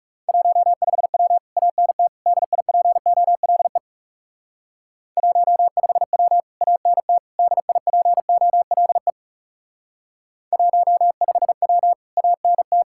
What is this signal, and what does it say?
Morse code, heard as a keyed tone, about 38 wpm, 700 Hz. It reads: 15W ANT DIPOLE 15W ANT DIPOLE 15W ANT